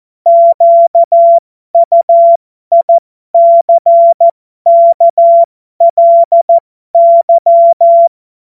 Morse code, heard as a keyed tone, about 14 wpm, spaced as for 12 wpm, 680 Hz. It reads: QUICKLY
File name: Morsekurs / Witze